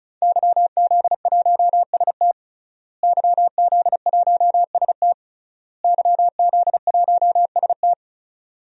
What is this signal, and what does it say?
Morse code, heard as a keyed tone, about 36 wpm, 700 Hz. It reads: YZ1ST YZ1ST YZ1ST
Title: DXCC-100 Call Signs - 5 Characters - Repeated 3x - 35wpm